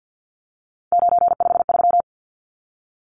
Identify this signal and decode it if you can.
Morse code, heard as a keyed tone, about 48 wpm, 700 Hz. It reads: 953
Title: Numbers - 3-Digit 50wpm